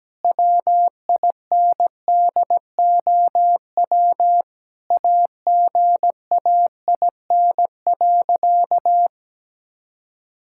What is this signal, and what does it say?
Morse code, heard as a keyed tone, about 17 wpm, 700 Hz. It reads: WINDOW AGAIN.
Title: Sentences from Top 1600 Words 17wpm